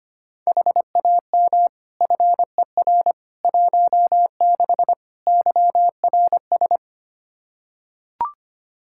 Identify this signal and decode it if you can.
Morse code, heard as a keyed tone, about 25 wpm, 700 Hz. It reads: HAM FER 16 YRS
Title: Practice QSO - Exchange 3 - More Info - 25wpm